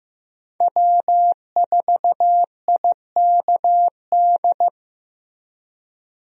Morse code, heard as a keyed tone, about 15 wpm, 700 Hz. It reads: W4IKD
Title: USA Call Signs - 5 Characters - Repeated 2x - 15wpm